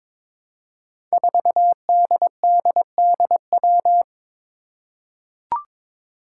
Morse code, heard as a keyed tone, about 22 wpm, 700 Hz. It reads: 4DDDW
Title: Five Challenging Letter-Number Combinations 22wpm